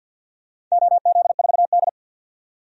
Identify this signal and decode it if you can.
Morse code, heard as a keyed tone, about 51 wpm, 700 Hz. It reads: OZ4D
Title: DXCC-100 Call Signs - 4 Characters - 50wpm